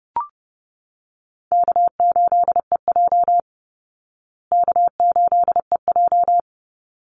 Morse code, heard as a keyed tone, about 30 wpm, 700 Hz. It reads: K8EJ K8EJ
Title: Call Signs Repeated 3x - 30wpm